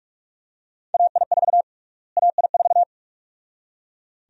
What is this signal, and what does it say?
Morse code, heard as a keyed tone, about 42 wpm, 700 Hz. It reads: AI4 AI4
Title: USA Call Sign Prefixes - Repeated 2x - 45wpm